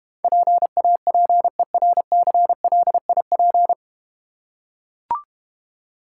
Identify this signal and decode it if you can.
Morse code, heard as a keyed tone, about 31 wpm, 700 Hz. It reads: PAPERCLIP
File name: Easy Words - Speed-Racing with Hints - 25wpm